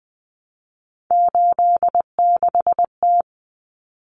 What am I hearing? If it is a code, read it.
Morse code, heard as a keyed tone, about 20 wpm, 700 Hz. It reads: 86T
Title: Common Cut Numbers - 3-Digit 20wpm